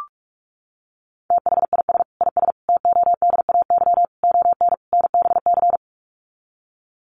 Morse code, heard as a keyed tone, about 44 wpm, 700 Hz. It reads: THIS IS TODAY ON NBC
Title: Sets of 5 Words Encore 45wpm